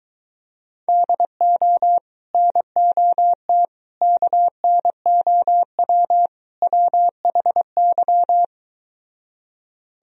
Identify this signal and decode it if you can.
Morse code, heard as a keyed tone, about 24 wpm, 700 Hz. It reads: DO NOT KNOW WHY